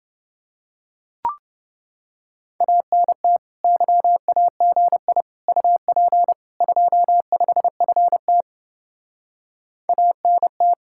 Morse code, heard as a keyed tone, about 30 wpm, 700 Hz. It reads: ANT YAGI UP 25FT ANT